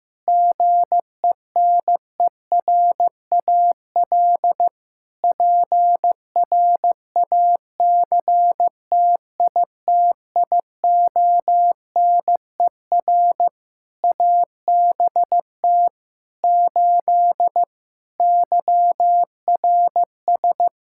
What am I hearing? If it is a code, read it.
Morse code, heard as a keyed tone, about 14 wpm, 700 Hz. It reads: GENERAL PRACTITIONER ABT 8 YRS
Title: Practice QSO - Exchange 3 - More Info - 15wpm